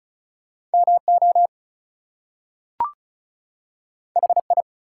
Morse code, heard as a keyed tone, about 36 wpm, 700 Hz. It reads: MO HI
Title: US States and Canadian Provinces 35wpm